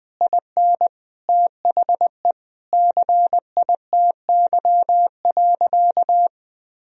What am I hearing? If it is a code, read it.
Morse code, heard as a keyed tone, about 20 wpm, 700 Hz. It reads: IN THE CITY.
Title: Sentences from Top 1000 Words - Review Part 3 - 20wpm